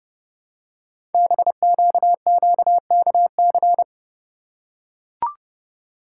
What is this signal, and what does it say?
Morse code, heard as a keyed tone, about 30 wpm, 690 Hz. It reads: BQQKC
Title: Code Groups - 5 Letters - 30wpm